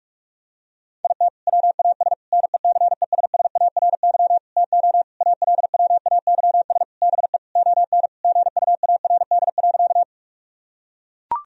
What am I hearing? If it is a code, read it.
Morse code, heard as a keyed tone, about 44 wpm, 700 Hz. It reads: IT WAS NECESSARY TO ALWAYS BE ON GUARD.
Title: Sentences from Top 1000 Words - Review Part 4 - 45wpm